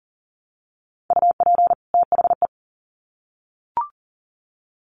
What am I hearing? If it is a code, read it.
Morse code, heard as a keyed tone, about 40 wpm, 700 Hz. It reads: UP THE
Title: Sets of 2 Words 40wpm